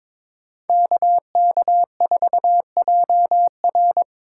Morse code, heard as a keyed tone, about 26 wpm, 700 Hz. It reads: KK4JR
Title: Speed Racing - USA 2x2 Call Signs 22wpm